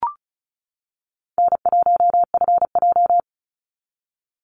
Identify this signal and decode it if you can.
Morse code, heard as a keyed tone, about 36 wpm, 700 Hz. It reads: N1FJ
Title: NA SOTA Activator Callsigns 35wpm